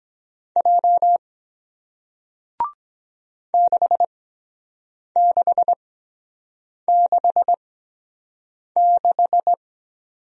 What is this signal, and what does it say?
Morse code, heard as a keyed tone, about 23 wpm, spaced as for 7 wpm, 700 Hz. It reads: J 6666